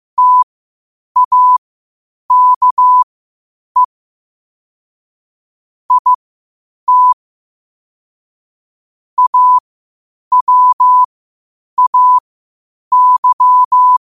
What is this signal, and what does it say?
Morse code, heard as a keyed tone, about 15 wpm, spaced as for 8 wpm, 1000 Hz. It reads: TAKE IT AWAY
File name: qotdfiles / 8wpm